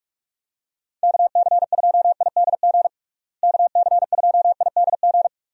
Morse code, heard as a keyed tone, about 41 wpm, 700 Hz. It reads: KC2IDG KC2IDG
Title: USA Call Signs - 6 Characters - Repeated 2x - 45wpm